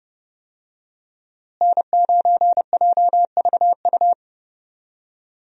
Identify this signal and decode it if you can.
Morse code, heard as a keyed tone, about 30 wpm, 700 Hz. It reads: N9JVU
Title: USA Call Signs - 5 Characters - Repeated 2x - 30wpm